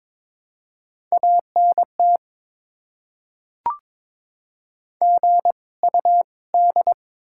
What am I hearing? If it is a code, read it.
Morse code, heard as a keyed tone, about 22 wpm, spaced as for 18 wpm, 690 Hz. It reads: ANT GUD
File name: Top 20 QSO Elements - 2x Character Spacing - 22wpm